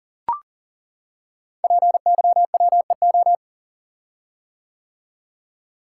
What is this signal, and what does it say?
Morse code, heard as a keyed tone, about 40 wpm, 700 Hz. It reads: PYWEO
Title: Code Groups - 5 Letters - 40wpm